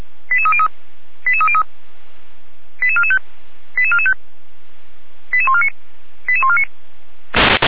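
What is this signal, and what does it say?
Five-tone selective call (ZVEI), heard as quick runs of five tones, tones of about 70 ms. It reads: 88303 88303 88445 88445 88149 88149
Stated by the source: ZVEI five-tone call received by SDR.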